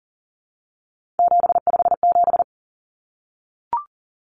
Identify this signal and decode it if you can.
Morse code, heard as a keyed tone, about 40 wpm, 700 Hz. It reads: 757
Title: Common Cut Numbers - 3-Digit 40wpm